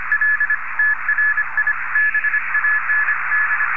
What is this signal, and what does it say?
Morse code, heard as a keyed tone, about 25 wpm, 1700 Hz, with radio noise. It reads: L TRI BAND
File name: General / 20M CW ADSP2-Good